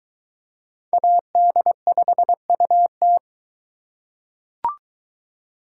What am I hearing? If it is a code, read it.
Morse code, heard as a keyed tone, about 21 wpm, 700 Hz. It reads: AD5UT